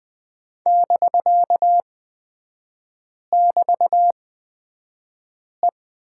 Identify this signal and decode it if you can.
Morse code, heard as a keyed tone, about 20 wpm, spaced as for 5 wpm, 700 Hz. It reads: <BK><BT>E
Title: 250 Lesson 40e - BK - Three Characters 20wpm